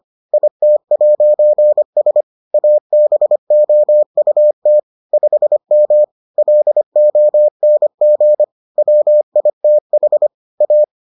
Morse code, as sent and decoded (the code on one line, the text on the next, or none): .. - .----. ... / .- -... --- ..- - / ..... -- / .-.. --- -. --. / .-- .. - .... / .-
IT'S ABOUT 5M LONG WITH A